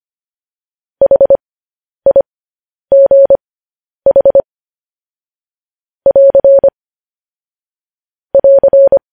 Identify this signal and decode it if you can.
Morse code, heard as a keyed tone, about 25 wpm, spaced as for 10 wpm, 550 Hz. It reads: HIGH <AR> <AR>
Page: news-headlines-in-morse-code-at-10-wpm